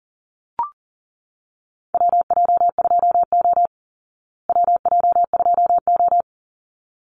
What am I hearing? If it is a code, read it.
Morse code, heard as a keyed tone, about 40 wpm, 700 Hz. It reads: WJ2O WJ2O
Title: Call Signs Repeated 3x - 40wpm